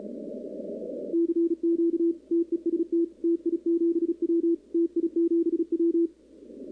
Morse code, heard as a keyed tone, about 35 wpm, 340 Hz, with radio noise. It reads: CQ TEST TI7W TI7W
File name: contests / 2019-cw-wpx-kt5j-k5pi- -2330 0-11